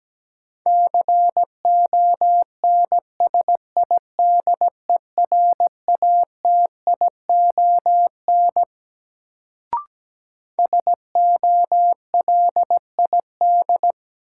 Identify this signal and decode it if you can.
Morse code, heard as a keyed tone, about 17 wpm, 700 Hz. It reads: CONSIDERATION SOLID
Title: Sentences from Top 1500 Words - New Word Drill Alt - 17wpm